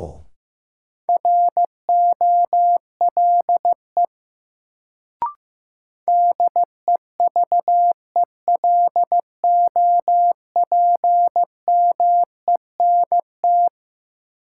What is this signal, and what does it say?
Morse code, heard as a keyed tone, about 15 wpm, 700 Hz. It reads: ROLE DEVELOPMENT
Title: Sets of 3 Words - New Word Drill - 15wpm